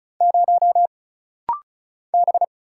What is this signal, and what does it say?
Morse code, heard as a keyed tone, about 34 wpm, 700 Hz. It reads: 0 B
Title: Single Letter-Number Alt - Rapid-Fire 35wpm